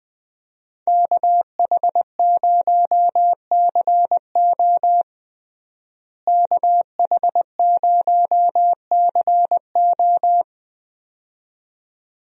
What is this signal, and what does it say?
Morse code, heard as a keyed tone, about 20 wpm, 700 Hz. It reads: KH0CO KH0CO
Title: USA Call Signs - 5 Characters - Repeated 2x - 20wpm